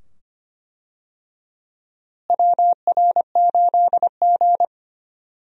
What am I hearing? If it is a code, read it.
Morse code, heard as a keyed tone, about 25 wpm, 700 Hz. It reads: WR8G
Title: USA Call Signs - 4 Characters - Repeated 2x - 25wpm